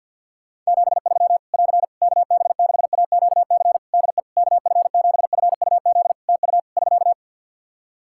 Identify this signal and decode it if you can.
Morse code, heard as a keyed tone, about 51 wpm, 700 Hz. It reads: 73 <AR> KB6AQC DE KF7FUZ TU <SK>
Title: Practice QSO - Exchange 4 - Ending - 50wpm